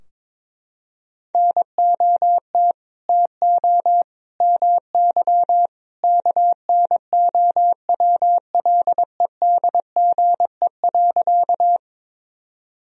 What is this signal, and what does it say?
Morse code, heard as a keyed tone, about 22 wpm, 700 Hz. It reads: NOT TO MY KNOWLEDGE.
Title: Sentences from Top 1000 Words - Review Part 3 - 22wpm